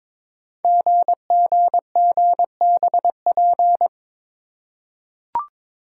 Morse code, heard as a keyed tone, about 22 wpm, 700 Hz. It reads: GGGBP
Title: Code Groups - 5 Letters - 22wpm